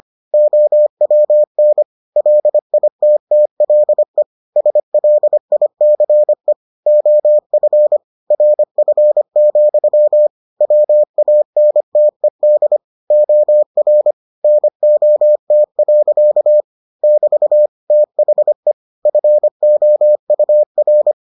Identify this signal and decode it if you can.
Morse code, heard as a keyed tone, about 25 wpm, 600 Hz. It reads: OWN LITTLE SLICE OF RF, WANTED OR NOT. <BT> THE FOUR